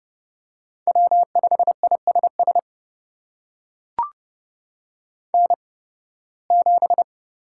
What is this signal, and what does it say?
Morse code, heard as a keyed tone, about 30 wpm, spaced as for 20 wpm, 700 Hz. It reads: W5ISS N7
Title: USA Call Signs - 5 Characters - 8x Character Spacing - 30wpm